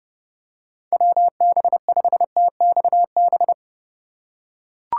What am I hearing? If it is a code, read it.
Morse code, heard as a keyed tone, about 30 wpm, 700 Hz. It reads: WB5TXB